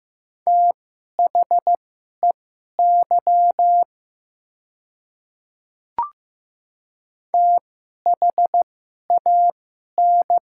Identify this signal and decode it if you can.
Morse code, heard as a keyed tone, about 15 wpm, spaced as for 11 wpm, 700 Hz. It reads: THEY THAN
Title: 2x Character Spacing - Top 100 Words - 15wpm